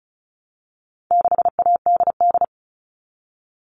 Morse code, heard as a keyed tone, about 34 wpm, 700 Hz. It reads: 6ADD
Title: All Cut Numbers - 4-Digit 35wpm